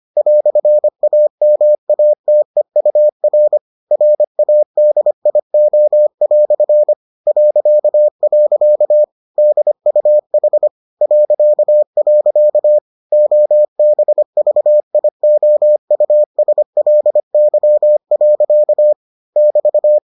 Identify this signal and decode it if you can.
Morse code, heard as a keyed tone, about 25 wpm, 600 Hz. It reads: "AMATEUR RADIO" .. DUH .. OBVIOUSLY. <BT>